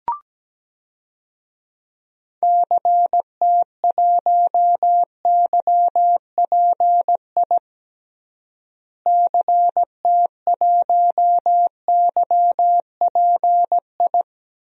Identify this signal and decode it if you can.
Morse code, heard as a keyed tone, about 17 wpm, 700 Hz. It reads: CT1YPI CT1YPI
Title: DXCC-100 Call Signs - 6 Characters - Repeated 3x - 17wpm